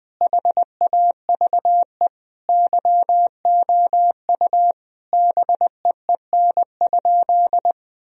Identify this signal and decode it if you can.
Morse code, heard as a keyed tone, about 21 wpm, 700 Hz. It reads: HAVE YOU BEEN?
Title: Sentences from Top 1000 Words - Review Part 3 - 20wpm